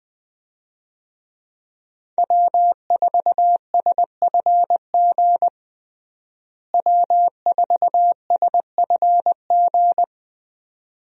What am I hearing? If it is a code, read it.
Morse code, heard as a keyed tone, about 20 wpm, 700 Hz. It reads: W4SFG W4SFG
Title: USA Call Signs - 5 Characters - Repeated 2x - 20wpm